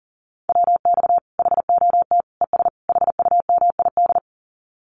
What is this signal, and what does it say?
Morse code, heard as a keyed tone, about 40 wpm, 700 Hz. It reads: WX HOT ES HUMID
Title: Common QSO Phrases 40wpm